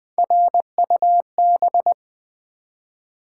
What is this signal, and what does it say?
Morse code, heard as a keyed tone, about 20 wpm, 700 Hz. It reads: RUB
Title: Three Letter Words - 20wpm